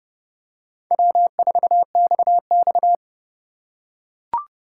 Morse code, heard as a keyed tone, about 28 wpm, 700 Hz. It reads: W4XX
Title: USA Call Signs - 3x Character Spacing - 30wpm